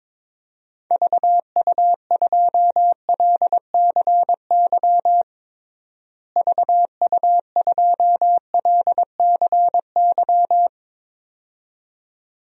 Morse code, heard as a keyed tone, about 22 wpm, 700 Hz. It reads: VU2LCY VU2LCY
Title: DXCC-100 Call Signs - 6 Characters - Repeated 2x - 22wpm